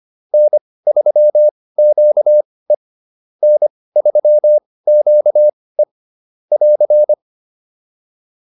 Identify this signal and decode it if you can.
Morse code, heard as a keyed tone, about 25 wpm, spaced as for 18 wpm, 600 Hz. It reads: N3QE N3QE <AR>